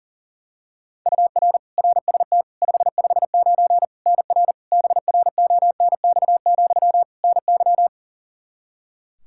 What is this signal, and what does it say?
Morse code, heard as a keyed tone, about 40 wpm, 700 Hz. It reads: UR RST 559 NR BRONX, NY
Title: K1USN SST QSO Practice 40wpm